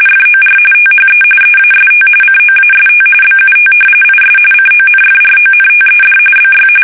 RTTY, about 45.45 baud. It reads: FOR THE EAST COAST BUT IS CONFRONTED BY
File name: fsk850 / battleshipnj